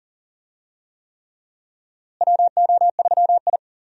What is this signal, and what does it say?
Morse code, heard as a keyed tone, about 40 wpm, 700 Hz. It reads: WO3I
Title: USA Call Signs - 4 Characters - Repeated 3x - 40wpm